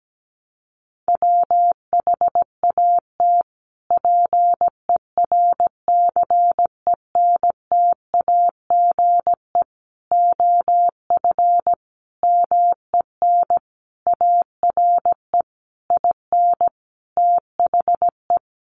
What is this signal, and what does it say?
Morse code, heard as a keyed tone, about 17 wpm, 700 Hz. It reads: WHAT PERCENTAGE OF MEN ARE IN THE
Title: Sentences from Top 1300 Words 17wpm